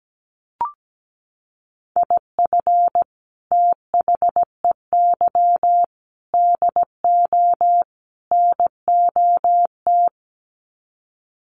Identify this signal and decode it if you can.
Morse code, heard as a keyed tone, about 17 wpm, 700 Hz. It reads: IF THEY DO NOT
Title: Sets of 4 Words 17wpm